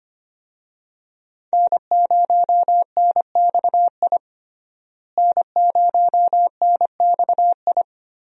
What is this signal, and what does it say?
Morse code, heard as a keyed tone, about 25 wpm, 700 Hz. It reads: N0NXI N0NXI
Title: USA Call Signs - 5 Characters - Repeated 3x - 25wpm